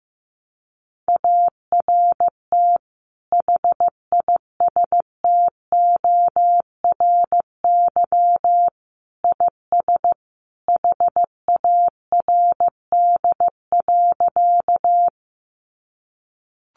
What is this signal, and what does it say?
Morse code, heard as a keyed tone, about 15 wpm, 700 Hz. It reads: ART HISTORY IS HARD.
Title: Sentences from Top 600 Words 15wpm